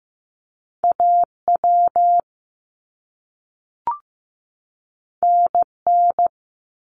Morse code, heard as a keyed tone, about 15 wpm, 700 Hz. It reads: AW NN
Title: Top Two Letter Combinations in Words 15wpm